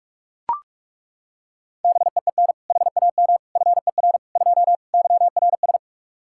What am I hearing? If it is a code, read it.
Morse code, heard as a keyed tone, about 41 wpm, 700 Hz. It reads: BEEN HAM FER 2 YRS
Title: Practice QSO - Exchange 3 - More Info - 45wpm